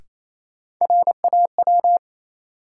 Morse code, heard as a keyed tone, about 28 wpm, 700 Hz. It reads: RAW
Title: Sentences from Top 1500 Words - New Word Drill - 28wpm